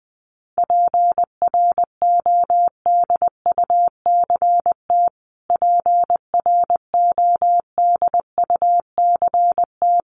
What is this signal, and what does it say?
Morse code, heard as a keyed tone, about 20 wpm, 700 Hz. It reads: PRODUCT PRODUCT
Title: Top 500 Words Repeated 20wpm